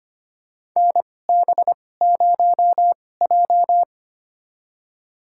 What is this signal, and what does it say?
Morse code, heard as a keyed tone, about 25 wpm, spaced as for 18 wpm, 700 Hz. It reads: NB0J
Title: USA Call Signs - 4 Characters - 2x Character Spacing - 25wpm